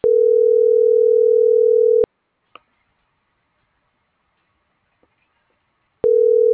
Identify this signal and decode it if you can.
Ringback tone (North America), 440 + 480 Hz.